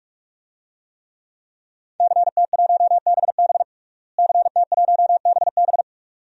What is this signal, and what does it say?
Morse code, heard as a keyed tone, about 46 wpm, 700 Hz. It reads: CT1BB CT1BB
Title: DXCC-100 Call Signs - 5 Characters - Repeated 2x - 45wpm